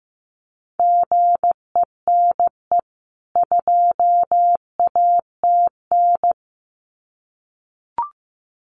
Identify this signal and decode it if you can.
Morse code, heard as a keyed tone, about 15 wpm, 700 Hz. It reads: GENE 2ATN
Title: CWT Exchange 15wpm